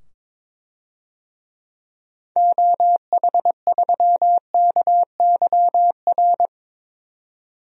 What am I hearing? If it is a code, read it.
Morse code, heard as a keyed tone, about 22 wpm, 700 Hz. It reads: OH3KYR
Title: DXCC-100 Call Signs - 6 Characters - 22wpm